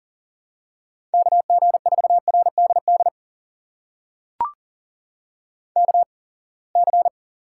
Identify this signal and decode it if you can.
Morse code, heard as a keyed tone, about 40 wpm, spaced as for 27 wpm, 700 Hz. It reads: KG4RDD KC
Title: USA Call Signs - 6 Characters - 8x Character Spacing - 40wpm